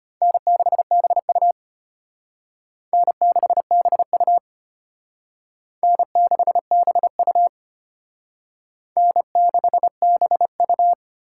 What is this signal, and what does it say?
Morse code, heard as a keyed tone, about 34 wpm, 700 Hz. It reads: N6BU N6BU N6BU N6BU